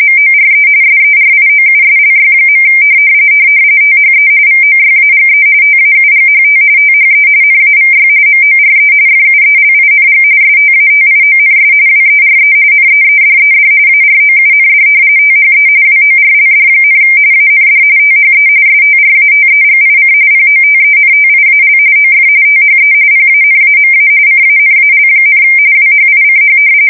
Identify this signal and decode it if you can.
RTTY, about 45.45 baud. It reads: MORE SERIOUSLY. THE VOYAGE TAUGHT THE NAVY MORE FUELING STATIONS WERE NEEDED AROUND THE WORLD, AND THE STRATEGIC POTENTIAL OF THE PANAMA CANAL, WHICH
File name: fsk170 / usnavy3